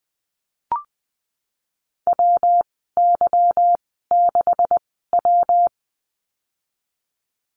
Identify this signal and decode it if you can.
Morse code, heard as a keyed tone, about 20 wpm, spaced as for 15 wpm, 700 Hz. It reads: WY6W
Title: 154 Lesson 25g - Letter p - Call Signs 20wpm